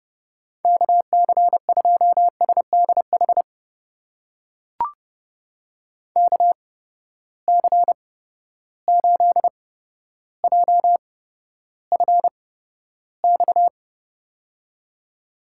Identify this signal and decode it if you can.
Morse code, heard as a keyed tone, about 30 wpm, spaced as for 13 wpm, 700 Hz. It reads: KC2SDH KC8JFX
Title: USA Call Signs - 8x Character Spacing - 30wpm